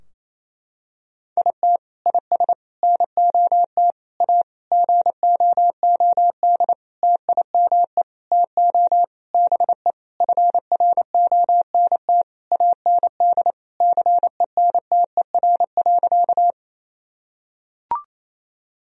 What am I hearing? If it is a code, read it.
Morse code, heard as a keyed tone, about 28 wpm, 700 Hz. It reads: IT IS NOT A GOOD TIME TO BE FRONT AND CENTER.
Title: Sentences from Top 1000 Words - Review Part 2 - 28wpm